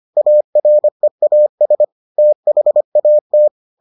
Morse code, heard as a keyed tone, about 25 wpm, 600 Hz. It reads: AREAS THAT